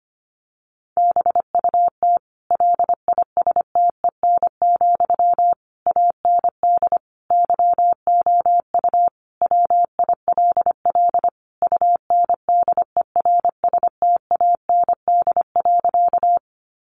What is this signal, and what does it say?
Morse code, heard as a keyed tone, about 25 wpm, 700 Hz. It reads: BUT LISTEN, AND YOU WILL UNDERSTAND.
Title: Sentences from Top 600 Words 25wpm